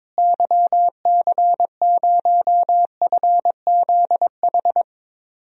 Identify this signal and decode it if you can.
Morse code, heard as a keyed tone, about 22 wpm, 700 Hz. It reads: YC0FZH
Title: DXCC-100 Call Signs - 6 Characters - Repeated 2x - 22wpm